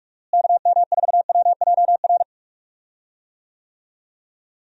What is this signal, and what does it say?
Morse code, heard as a keyed tone, about 42 wpm, 700 Hz. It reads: KM4WJR
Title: USA Call Signs - 6 Characters - Repeated 2x - 45wpm